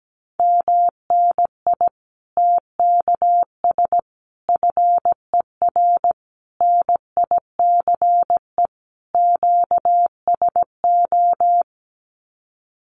Morse code, heard as a keyed tone, about 17 wpm, 700 Hz. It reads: MNI TKS FER NICE QSO
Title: Common QSO Phrases 17wpm